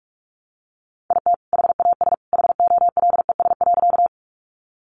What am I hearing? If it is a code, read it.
Morse code, heard as a keyed tone, about 44 wpm, 700 Hz. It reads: IT HAS HOLES.
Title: Additional Easy Phrases 45wpm